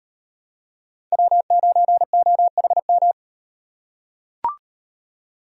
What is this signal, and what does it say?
Morse code, heard as a keyed tone, about 38 wpm, 700 Hz. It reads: W9OHM